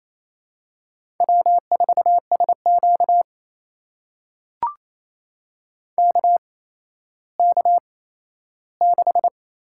Morse code, heard as a keyed tone, about 28 wpm, spaced as for 14 wpm, 700 Hz. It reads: W4SQ KK6
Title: USA Call Signs - 4 Characters - 8x Character Spacing - 28wpm